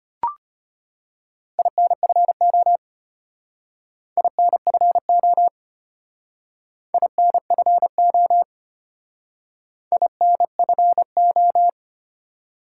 Morse code, heard as a keyed tone, about 32 wpm, 700 Hz. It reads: INFO INFO INFO INFO